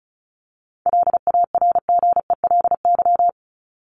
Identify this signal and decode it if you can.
Morse code, heard as a keyed tone, about 34 wpm, 700 Hz. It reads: LARGELY
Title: LY Suffix - Top 100 Words with Base Words 35wpm